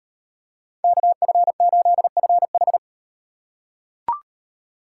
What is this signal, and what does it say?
Morse code, heard as a keyed tone, about 38 wpm, 700 Hz. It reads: KF8FH